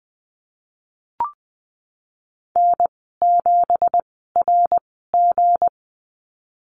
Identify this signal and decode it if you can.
Morse code, heard as a keyed tone, about 20 wpm, spaced as for 15 wpm, 700 Hz. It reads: N7RG
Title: 167 Lesson 27g - Number 7 - Call Signs 20wpm